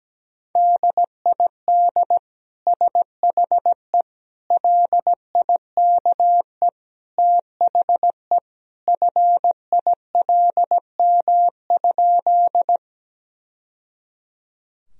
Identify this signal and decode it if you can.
Morse code, heard as a keyed tone, about 17 wpm, 700 Hz. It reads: DID SHE LIKE THE FILM?
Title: Sentences from Top 1000 Words - Review Part 3 - 17wpm